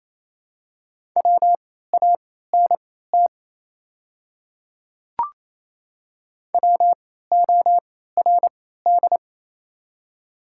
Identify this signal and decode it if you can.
Morse code, heard as a keyed tone, about 28 wpm, spaced as for 16 wpm, 690 Hz. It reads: WANT WORD
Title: 3x Character Spacing - Top 200 Words - 28wpm